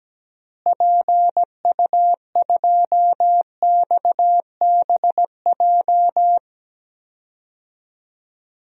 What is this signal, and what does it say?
Morse code, heard as a keyed tone, about 17 wpm, 700 Hz. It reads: PU2XBJ
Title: DXCC-100 Call Signs - 6 Characters - Repeated 2x - 17wpm